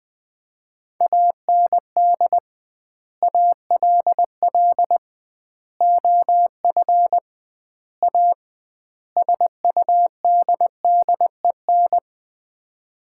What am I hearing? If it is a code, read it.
Morse code, heard as a keyed tone, about 20 wpm, 700 Hz. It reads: AND ALL OF A SUDDEN
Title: 2x Word-Spacing - Sets of 5 Words - 20wpm